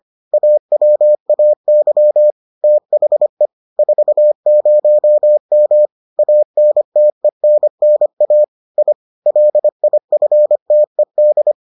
Morse code, heard as a keyed tone, about 25 wpm, 600 Hz. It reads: AWAY THE 40M ANTENNA I LIFTED